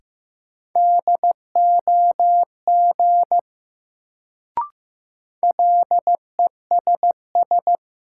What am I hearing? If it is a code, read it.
Morse code, heard as a keyed tone, about 15 wpm, 700 Hz. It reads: DOG LESS
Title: Sentences from Top 400 Words - New Word Drill - 15wpm